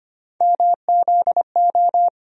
Morse code, heard as a keyed tone, about 25 wpm, 700 Hz. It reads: MZO
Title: USA Call Sign Postfixes - 3 Letters 25wpm